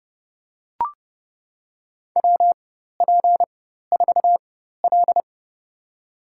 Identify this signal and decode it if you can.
Morse code, heard as a keyed tone, about 30 wpm, spaced as for 14 wpm, 700 Hz. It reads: WP4L
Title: USA Call Signs - 4x Character Spacing - 30wpm